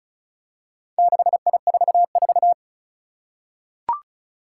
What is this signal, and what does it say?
Morse code, heard as a keyed tone, about 36 wpm, 700 Hz. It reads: 6I44
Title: Four Challenging Letter-Number Combinations 35wpm